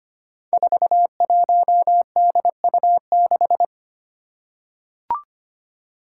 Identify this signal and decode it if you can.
Morse code, heard as a keyed tone, about 25 wpm, 700 Hz. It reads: 41DU6
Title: Five Challenging Letter-Number Combinations 25wpm